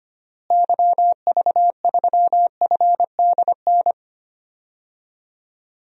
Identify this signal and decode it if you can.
Morse code, heard as a keyed tone, about 25 wpm, 700 Hz. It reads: YV3FDN
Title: DXCC-100 Call Signs - 6 Characters - Repeated 3x - 25wpm